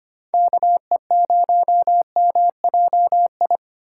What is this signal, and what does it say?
Morse code, heard as a keyed tone, about 25 wpm, 700 Hz. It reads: KE0MJI
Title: USA Call Signs - 3x Character Spacing - 25wpm